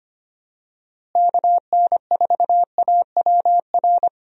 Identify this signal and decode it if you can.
Morse code, heard as a keyed tone, about 25 wpm, 700 Hz. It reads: KN4AWR
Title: USA Call Signs - 6 Characters - Repeated 2x - 25wpm